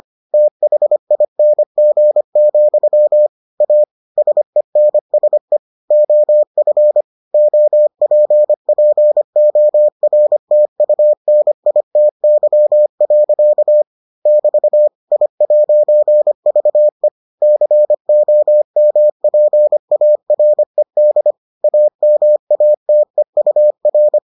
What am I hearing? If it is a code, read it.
Morse code, heard as a keyed tone, about 25 wpm, 600 Hz. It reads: THING, A SENSE OF OPPORTUNITY. <BT> I'VE COMPARED AMATEUR